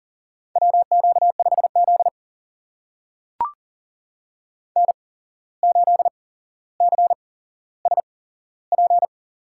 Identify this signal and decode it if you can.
Morse code, heard as a keyed tone, about 40 wpm, spaced as for 16 wpm, 700 Hz. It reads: WQ5Z N8CSP
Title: USA Call Signs - 8x Character Spacing - 40wpm